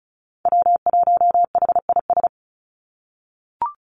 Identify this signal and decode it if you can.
Morse code, heard as a keyed tone, about 34 wpm, 700 Hz. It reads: W1HIS
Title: Call Signs - 35wpm